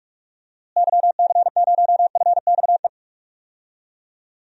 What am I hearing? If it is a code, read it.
Morse code, heard as a keyed tone, about 46 wpm, 700 Hz. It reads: YC0FXE
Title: DXCC-100 Call Signs - 6 Characters - Repeated 2x - 45wpm